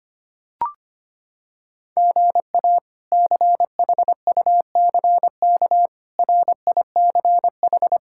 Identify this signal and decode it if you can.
Morse code, heard as a keyed tone, about 25 wpm, 700 Hz. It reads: GA CHUCK RICH
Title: K1USN SST QSO Practice - 25wpm